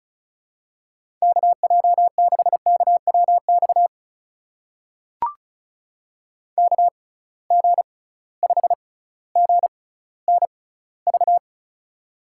Morse code, heard as a keyed tone, about 36 wpm, spaced as for 18 wpm, 700 Hz. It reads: KJ6KWX KG5GNV
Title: USA Call Signs - 6 Characters - 6x Character Spacing - 35wpm